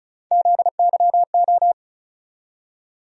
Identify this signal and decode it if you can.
Morse code, heard as a keyed tone, about 36 wpm, 700 Hz. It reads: ZYO
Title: USA Call Sign Postfixes - 3 Letters 35wpm